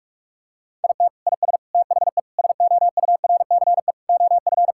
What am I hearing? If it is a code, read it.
Morse code, heard as a keyed tone, about 42 wpm, 700 Hz. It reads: IT IS THE SOURCE OF
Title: Sentences from Top 1000 Words - Review Part 5 - 45wpm